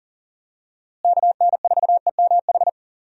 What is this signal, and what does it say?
Morse code, heard as a keyed tone, about 40 wpm, 700 Hz. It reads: KN4EMH
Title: USA Call Signs - 6 Characters - Repeated 3x - 40wpm